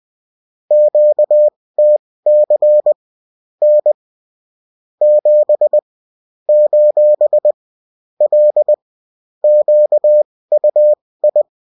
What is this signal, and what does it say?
Morse code, heard as a keyed tone, about 20 wpm, spaced as for 16 wpm, 600 Hz. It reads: QTC N 7 : L QUI